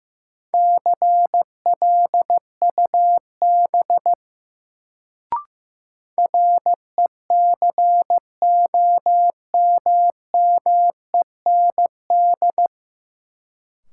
Morse code, heard as a keyed tone, about 15 wpm, 700 Hz. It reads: CLUB RECOMMEND
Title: Sentences from Top 1200 Words - New Word Drill Alt - 15wpm